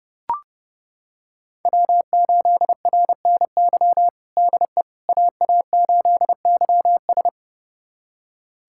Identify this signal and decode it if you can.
Morse code, heard as a keyed tone, about 30 wpm, 700 Hz. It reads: W8RNY DE AA8YS
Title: USA Callsign DE Callsign 30wpm